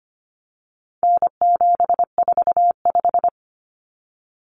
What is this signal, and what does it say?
Morse code, heard as a keyed tone, about 25 wpm, 700 Hz. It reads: N745
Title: Common Cut Numbers - 4-Digit 25wpm